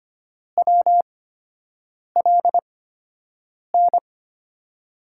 Morse code, heard as a keyed tone, about 25 wpm, spaced as for 7 wpm, 700 Hz. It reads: WLN
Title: USA Call Signs - 6 Characters - 8x Character Spacing - 25wpm